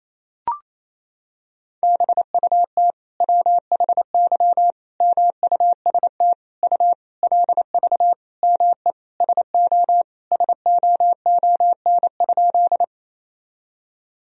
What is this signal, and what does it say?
Morse code, heard as a keyed tone, about 28 wpm, 700 Hz. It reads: BUT WHY MUST U LV ME SO SOON?
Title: Sentences from Top 400 Words and Abbreviations 28wpm